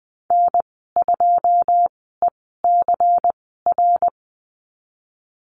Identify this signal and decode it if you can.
Morse code, heard as a keyed tone, about 20 wpm, spaced as for 14 wpm, 700 Hz. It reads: N2ECR
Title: 140 Lesson 23g - Letter f - Call Signs 20wpm